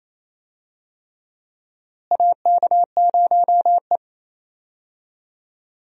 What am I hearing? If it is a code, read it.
Morse code, heard as a keyed tone, about 28 wpm, 700 Hz. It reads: AK0E
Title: USA Call Signs - 4 Characters - Repeated 3x - 28wpm